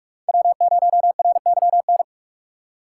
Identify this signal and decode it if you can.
Morse code, heard as a keyed tone, about 44 wpm, 700 Hz. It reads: W0RYN
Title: USA Call Signs - 5 Characters - 6x Character Spacing - 45wpm